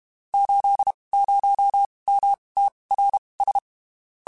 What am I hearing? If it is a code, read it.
Morse code, heard as a keyed tone, about 32 wpm, spaced as for 23 wpm, 800 Hz. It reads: 80MTRS
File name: (spaced) / QSOs Spaced-mp3s / WD0FBT 1 @30 sp